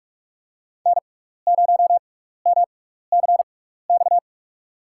Morse code, heard as a keyed tone, about 46 wpm, spaced as for 15 wpm, 700 Hz. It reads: N0MCX
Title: USA Call Signs - 5 Characters - 6x Character Spacing - 45wpm